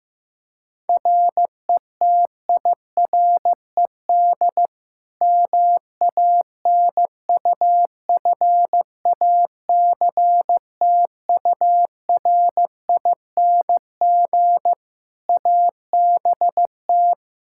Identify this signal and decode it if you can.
Morse code, heard as a keyed tone, about 14 wpm, 700 Hz. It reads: RETIRED MANUFACTURING ABT
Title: Practice QSO - Exchange 3 - More Info - 15wpm